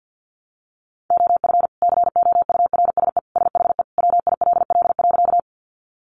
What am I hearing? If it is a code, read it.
Morse code, heard as a keyed tone, about 50 wpm, 700 Hz. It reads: OF COURSE SHE WILL.
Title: Sentences from Top 500 Words 50wpm